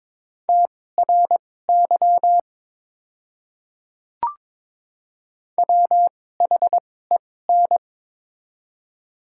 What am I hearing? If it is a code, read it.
Morse code, heard as a keyed tone, about 22 wpm, spaced as for 16 wpm, 700 Hz. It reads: TRY WHEN
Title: 2x Character Spacing - Top 500 Words - 22wpm